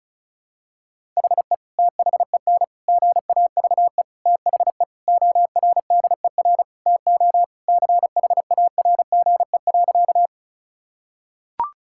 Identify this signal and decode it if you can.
Morse code, heard as a keyed tone, about 34 wpm, 700 Hz. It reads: HE THEN GAVE THE ORDER TO CHARGE.
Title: Sentences from Top 1000 Words - Review Part 6 - 35wpm